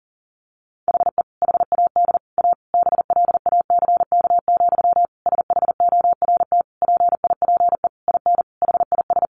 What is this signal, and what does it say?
Morse code, heard as a keyed tone, about 40 wpm, 700 Hz. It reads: HE HAD A BLACK, SHORT PIPE IN HIS
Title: Sentences from Top 1500 Words 40wpm